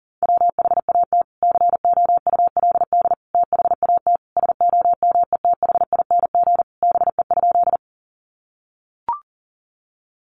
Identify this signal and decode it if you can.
Morse code, heard as a keyed tone, about 40 wpm, 700 Hz. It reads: WHAT COULD THAT SOMETHING BE?
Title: Sentences from Top 600 Words 40wpm